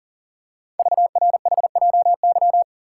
Morse code, heard as a keyed tone, about 39 wpm, 690 Hz. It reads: VRHJY
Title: Code Groups - 5 Letter:Numbers - 40wpm